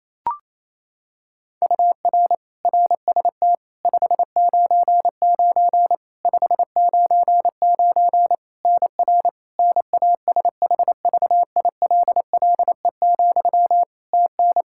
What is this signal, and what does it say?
Morse code, heard as a keyed tone, about 28 wpm, 700 Hz. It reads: UR RST 599 599 NR NASHVILLE, TN
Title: Practice QSO - Exchange 1 - Beginning - 28wpm